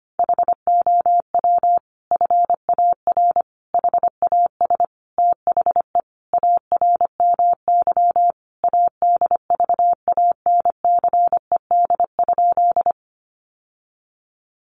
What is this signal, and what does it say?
Morse code, heard as a keyed tone, about 25 wpm, 700 Hz. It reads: HOW FAR HAS THE ARMY ADVANCED?
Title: Sentences from Top 1200 Words 25wpm